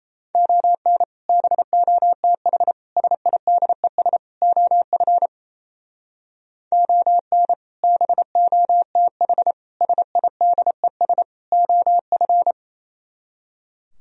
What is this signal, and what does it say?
Morse code, heard as a keyed tone, about 34 wpm, 700 Hz. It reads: ON BOTH SIDES OF ON BOTH SIDES OF
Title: Speed-Racing - Sets of 4 Words 28wpm